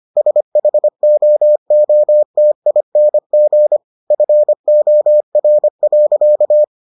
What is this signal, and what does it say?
Morse code, heard as a keyed tone, about 25 wpm, 600 Hz. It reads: SHOOTING FOR.